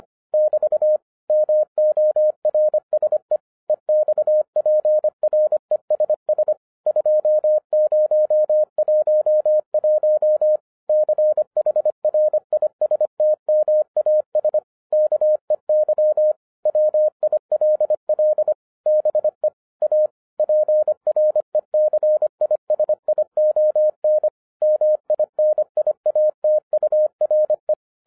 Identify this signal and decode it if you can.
Morse code, heard as a keyed tone, about 25 wpm, 610 Hz. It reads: <BT> MORSE EXPRESS 2011 CHRISTMAS KEY WILL BE A PRECISION MINIATURE
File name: 2011-10-mp3 / n14-MORSE EXPRESS 2011 CHRISTMAS KEY 25wpm